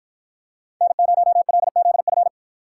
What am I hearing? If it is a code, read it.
Morse code, heard as a keyed tone, about 53 wpm, 700 Hz. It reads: N0LZF